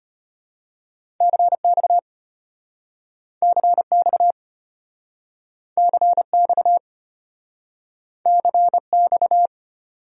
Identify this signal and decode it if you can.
Morse code, heard as a keyed tone, about 33 wpm, 700 Hz. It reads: CX CX CX CX